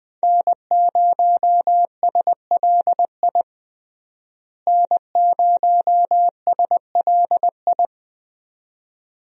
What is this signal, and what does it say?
Morse code, heard as a keyed tone, about 20 wpm, 700 Hz. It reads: N0SLI N0SLI
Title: USA Call Signs - 5 Characters - Repeated 2x - 20wpm